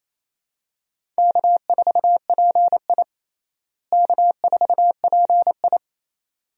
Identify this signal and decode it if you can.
Morse code, heard as a keyed tone, about 27 wpm, 700 Hz. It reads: K4PI K4PI
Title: USA Call Signs - 4 Characters - Repeated 2x - 28wpm